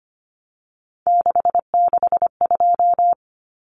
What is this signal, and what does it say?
Morse code, heard as a keyed tone, about 25 wpm, 690 Hz. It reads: 662
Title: Numbers - 3-Digit 25wpm